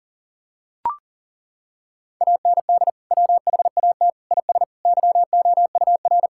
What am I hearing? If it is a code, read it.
Morse code, heard as a keyed tone, about 40 wpm, 700 Hz. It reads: AND WHAT IS YOUR
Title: Sentences from Top 1600 Words 40wpm